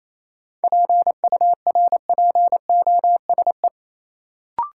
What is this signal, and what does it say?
Morse code, heard as a keyed tone, about 28 wpm, 700 Hz. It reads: PURPOSE
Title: Sentences from Top 800 Words - New Word Drill - 28wpm